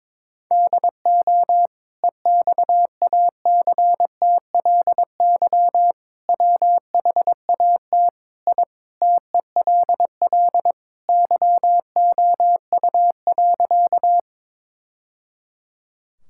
Morse code, heard as a keyed tone, about 22 wpm, 700 Hz. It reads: DO EXACTLY WHAT I TELL YOU.
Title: Sentences from Top 1000 Words - Review Part 6 - 22wpm